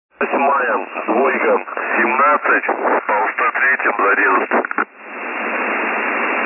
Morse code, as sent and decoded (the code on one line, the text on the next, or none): none